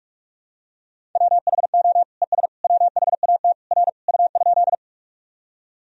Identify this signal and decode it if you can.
Morse code, heard as a keyed tone, about 46 wpm, 700 Hz. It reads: WHO ES WHAT R U?
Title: Sentences from Top 100 Words and Abbreviations 45wpm